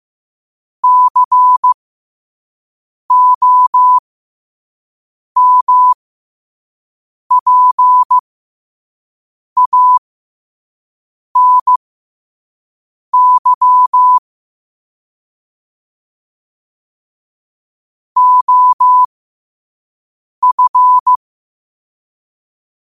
Morse code, heard as a keyed tone, about 15 wpm, spaced as for 5 wpm, 1000 Hz. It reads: COMPANY OF
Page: New quotes every day in morse code at 5 Words per minute.